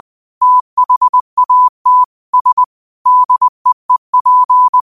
Morse code, heard as a keyed tone, about 20 wpm, 1000 Hz. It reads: THAT S DEEP